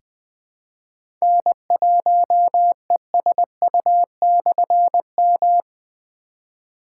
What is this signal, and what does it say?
Morse code, heard as a keyed tone, about 20 wpm, 700 Hz. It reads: N1ESU/M
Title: Difficult Call Signs 20wpm